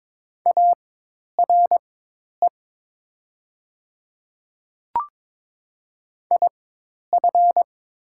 Morse code, heard as a keyed tone, about 22 wpm, spaced as for 10 wpm, 700 Hz. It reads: ARE IF
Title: 4x Character Spacing - Top 100 Words - 22wpm